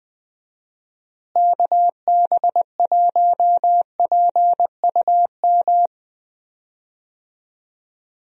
Morse code, heard as a keyed tone, about 20 wpm, 700 Hz. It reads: KB1PUM
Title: USA Call Signs - 6 Characters - Repeated 2x - 20wpm